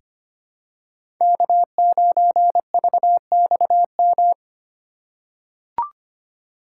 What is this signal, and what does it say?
Morse code, heard as a keyed tone, about 25 wpm, 700 Hz. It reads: K9VXM
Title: USA Call Signs - 5 Characters - 3x Character Spacing - 25wpm